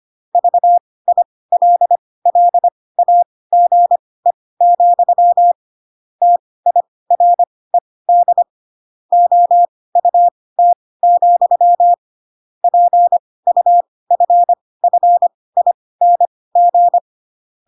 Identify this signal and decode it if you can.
Morse code, heard as a keyed tone, about 25 wpm, spaced as for 18 wpm, 700 Hz. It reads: VILLAGE, TIRED OUT, PUFFING
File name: Pinocchio / Pinocchio ch6 12 25 ik1hge